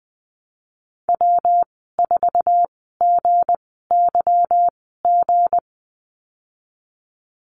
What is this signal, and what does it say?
Morse code, heard as a keyed tone, about 20 wpm, spaced as for 15 wpm, 700 Hz. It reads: W4GYG
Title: 161 Lesson 26g - Letter g - Call Signs 20wpm